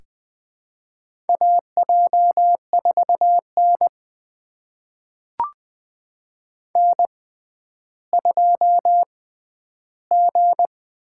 Morse code, heard as a keyed tone, about 20 wpm, spaced as for 11 wpm, 700 Hz. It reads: AJ4N N2G
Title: USA Call Signs - 6x Character Spacing - 20wpm